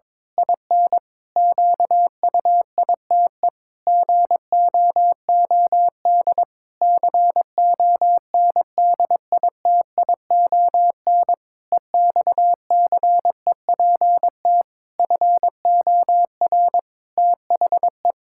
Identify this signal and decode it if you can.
Morse code, heard as a keyed tone, about 22 wpm, 700 Hz. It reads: IN QUITE GOOD CONDITION EXCEPT FOR THE
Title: Pangram Challenge v2 - 22wpm